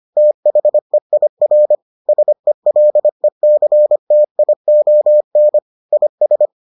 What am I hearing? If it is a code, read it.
Morse code, heard as a keyed tone, about 25 wpm, 600 Hz. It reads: THEIR SELECTION IS